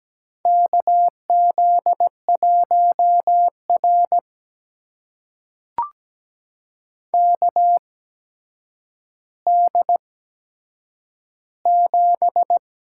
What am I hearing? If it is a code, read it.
Morse code, heard as a keyed tone, about 17 wpm, spaced as for 8 wpm, 700 Hz. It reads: KZ1R KD7
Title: USA Call Signs - 8x Character Spacing - 17wpm